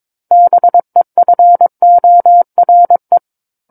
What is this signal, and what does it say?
Morse code, heard as a keyed tone, about 22 wpm, 700 Hz. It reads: BEFORE
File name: Book One / 13-Chapter 13 - 22 WPM